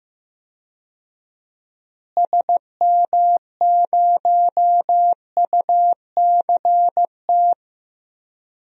Morse code, heard as a keyed tone, about 15 wpm, 700 Hz. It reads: SM0UCT
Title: DXCC-100 Call Signs - 6 Characters - Repeated 3x - 15wpm